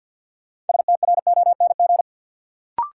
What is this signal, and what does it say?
Morse code, heard as a keyed tone, about 48 wpm, 700 Hz. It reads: STRONG
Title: Sentences from Top 500 Words - New Word Drill - 50wpm